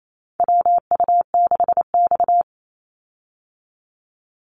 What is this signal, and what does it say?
Morse code, heard as a keyed tone, about 28 wpm, 700 Hz. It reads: WU6X
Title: Call Signs Repeated 2x - 28wpm